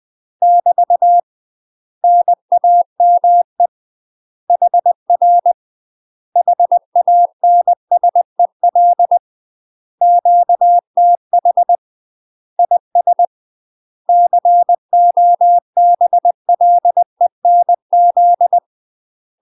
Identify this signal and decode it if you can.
Morse code, heard as a keyed tone, about 20 wpm, 700 Hz. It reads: <BT> NAME HR HANSEL QTH IS COBLENZ
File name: UK MASTER QSOs TRAINING/20WPM